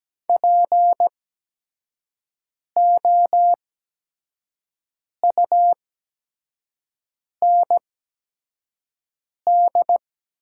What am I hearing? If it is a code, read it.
Morse code, heard as a keyed tone, about 17 wpm, spaced as for 5 wpm, 700 Hz. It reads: POUND
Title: 8x Character Spacing - Top 500 Words - 17wpm